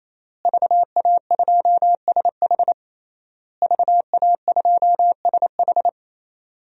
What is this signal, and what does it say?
Morse code, heard as a keyed tone, about 28 wpm, 700 Hz. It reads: VA2SH VA2SH
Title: DXCC-100 Call Signs - 5 Characters - Repeated 3x - 28wpm